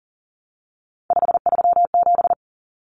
Morse code, heard as a keyed tone, about 40 wpm, 700 Hz. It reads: 537
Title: Common Cut Numbers - 3-Digit 40wpm